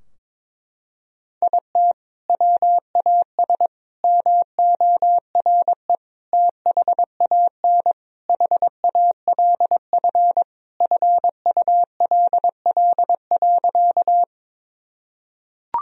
Morse code, heard as a keyed tone, about 22 wpm, 700 Hz. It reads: IT WAS MORE THAN HALF FULL.
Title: Sentences from Top 1000 Words - Review Part 3 - 22wpm